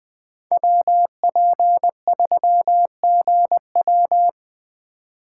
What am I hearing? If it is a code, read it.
Morse code, heard as a keyed tone, about 21 wpm, 700 Hz. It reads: WP3GW
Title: USA Call Signs - 3x Character Spacing - 20wpm